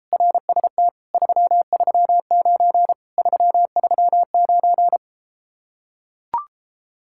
Morse code, heard as a keyed tone, about 33 wpm, 700 Hz. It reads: RST 339 339